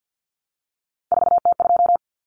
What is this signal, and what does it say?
Morse code, heard as a keyed tone, about 48 wpm, 700 Hz. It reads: 4T2
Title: Common Cut Numbers - 3-Digit 50wpm